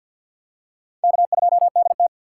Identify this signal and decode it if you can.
Morse code, heard as a keyed tone, about 50 wpm, 700 Hz. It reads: K2DT
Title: USA Call Signs - 4 Characters - 6x Character Spacing - 50wpm